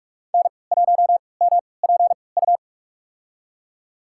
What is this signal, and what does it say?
Morse code, heard as a keyed tone, about 46 wpm, spaced as for 26 wpm, 700 Hz. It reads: N1MPU
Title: USA Call Signs - 5 Characters - 3x Character Spacing - 45wpm